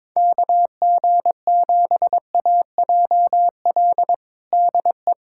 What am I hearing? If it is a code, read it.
Morse code, heard as a keyed tone, about 22 wpm, 700 Hz. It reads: KG7AJL DE
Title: USA Callsign DE Callsign - 6 Characters - 22wpm